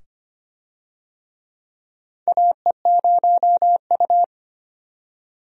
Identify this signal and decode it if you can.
Morse code, heard as a keyed tone, about 25 wpm, 700 Hz. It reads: AE0U
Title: USA Call Signs - 4 Characters - 25wpm